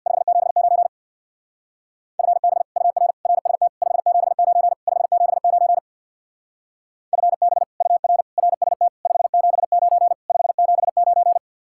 Morse code, heard as a keyed tone, about 62 wpm, 700 Hz. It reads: 579 FB UR RST 579 579 FB UR RST 579 579
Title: Speed-Racing Common QSO Phrases - 50wpm